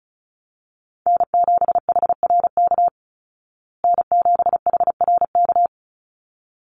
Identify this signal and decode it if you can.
Morse code, heard as a keyed tone, about 34 wpm, 700 Hz. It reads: N7HRK N7HRK
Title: Call Signs Repeated 3x - 35wpm